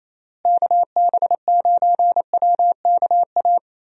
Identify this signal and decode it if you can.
Morse code, heard as a keyed tone, about 27 wpm, 700 Hz. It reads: KB9WKA
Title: USA Call Signs - 6 Characters - Repeated 3x - 28wpm